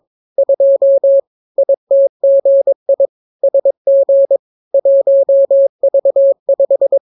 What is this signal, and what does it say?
Morse code, heard as a keyed tone, about 22 wpm, 550 Hz. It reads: 2 ITGI SG 1V5